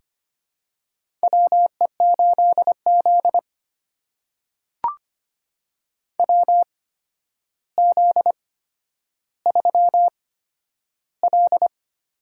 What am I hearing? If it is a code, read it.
Morse code, heard as a keyed tone, about 25 wpm, spaced as for 11 wpm, 700 Hz. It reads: WE8Z WZ3L
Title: USA Call Signs - 4 Characters - 8x Character Spacing - 25wpm